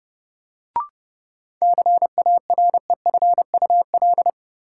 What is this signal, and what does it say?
Morse code, heard as a keyed tone, about 30 wpm, 700 Hz. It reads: CAREFUL
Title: Sentences from Top 1600 Words - New Word Drill - 30wpm